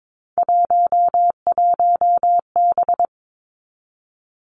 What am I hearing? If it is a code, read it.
Morse code, heard as a keyed tone, about 22 wpm, 700 Hz. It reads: 11B
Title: Three Challenging Letter-Number Combinations 22wpm